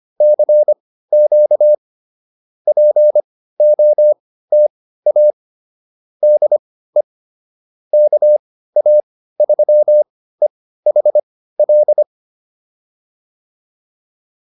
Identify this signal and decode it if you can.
Morse code, heard as a keyed tone, about 25 wpm, spaced as for 15 wpm, 590 Hz. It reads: CQ POTA DE KA3EHL